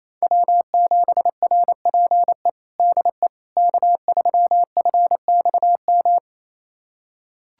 Morse code, heard as a keyed tone, about 28 wpm, 700 Hz. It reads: W7RPE DE K3FXM
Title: USA Callsign DE Callsign - 5 Characters - 28wpm